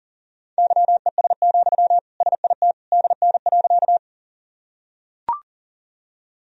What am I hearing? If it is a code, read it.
Morse code, heard as a keyed tone, about 40 wpm, 700 Hz. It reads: YES, SIT DN.
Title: Sentences from Top 500 Words and Abbreviations 40wpm